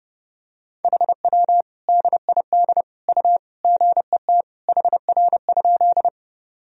Morse code, heard as a keyed tone, about 30 wpm, 700 Hz. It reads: HW DID U GET HR?
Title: Sentences from Top 200 Words and Abbreviations 30wpm